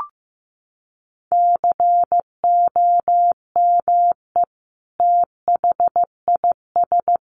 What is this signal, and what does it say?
Morse code, heard as a keyed tone, about 15 wpm, 700 Hz. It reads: COME THIS
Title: Sentences from Top 100 Words 15wpm